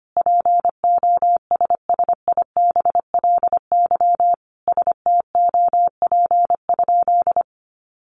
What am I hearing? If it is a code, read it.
Morse code, heard as a keyed tone, about 25 wpm, 700 Hz. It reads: POSSIBLY STOP?
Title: Sentences from Top 1300 Words 25wpm